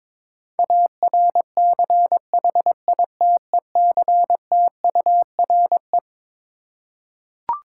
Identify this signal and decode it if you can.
Morse code, heard as a keyed tone, about 22 wpm, 700 Hz. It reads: ARCHITECTURE
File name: Twelve Letter Words - 22wpm